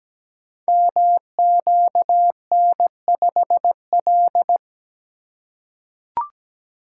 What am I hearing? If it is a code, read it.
Morse code, heard as a keyed tone, about 17 wpm, 690 Hz. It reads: MQN5L
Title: Code Groups - 5 Letter:Numbers - 17wpm